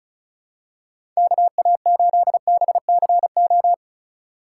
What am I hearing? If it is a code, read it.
Morse code, heard as a keyed tone, about 36 wpm, 700 Hz. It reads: KA8BCO
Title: USA Call Signs - 6 Characters - 4x Character Spacing - 35wpm